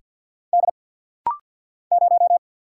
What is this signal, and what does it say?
Morse code, heard as a keyed tone, about 49 wpm, 690 Hz. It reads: D 0
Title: Single Letter-Number Alt - Rapid-Fire 50wpm